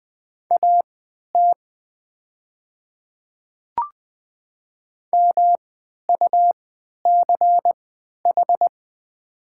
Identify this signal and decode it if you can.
Morse code, heard as a keyed tone, about 20 wpm, spaced as for 11 wpm, 700 Hz. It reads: AT MUCH
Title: 3x Character Spacing - Top 200 Words - 20wpm